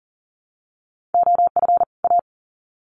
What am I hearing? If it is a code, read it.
Morse code, heard as a keyed tone, about 40 wpm, 700 Hz. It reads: OF A
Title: Sets of 2 Words 40wpm